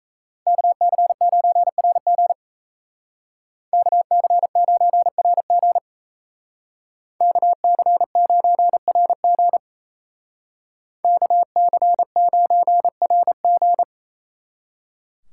Morse code, heard as a keyed tone, about 34 wpm, 700 Hz. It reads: KC9RG KC9RG KC9RG KC9RG
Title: Speed Racing - USA Call Signs 28wpm